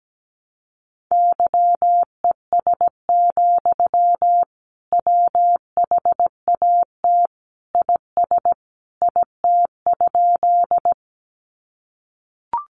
Morse code, heard as a keyed tone, about 17 wpm, 700 Hz. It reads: YES, WHAT IS IT?
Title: Sentences from Top 500 Words 17wpm